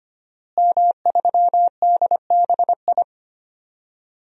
Morse code, heard as a keyed tone, about 25 wpm, 700 Hz. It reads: M3DBI
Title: DXCC-100 Call Signs - 5 Characters - Repeated 2x - 25wpm